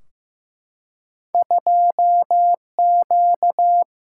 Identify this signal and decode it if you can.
Morse code, heard as a keyed tone, about 15 wpm, 700 Hz. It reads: 2Q
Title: Single-Letter Single-Number 15wpm